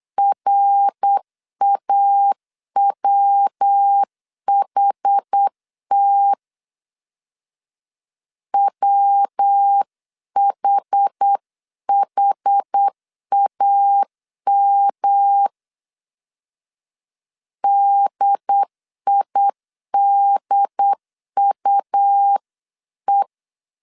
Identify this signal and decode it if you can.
Morse code, heard as a keyed tone, about 8 wpm, 800 Hz. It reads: RAWHT WHHAM DIDUE